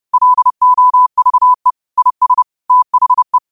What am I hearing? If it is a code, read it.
Morse code, heard as a keyed tone, about 30 wpm, 1000 Hz. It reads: LOVE IS THE
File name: qotdfiles / 30wpm